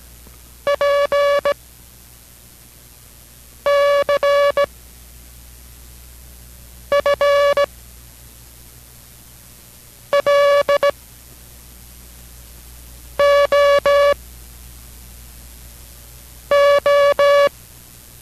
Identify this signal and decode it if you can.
Morse code, heard as a keyed tone, about 17 wpm, spaced as for 4 wpm, 570 Hz. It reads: PCFLOO